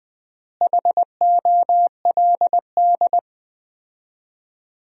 Morse code, heard as a keyed tone, about 20 wpm, 700 Hz. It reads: HOLD
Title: 4x Word-Spacing - Sentences from Top 400 Words - 20wpm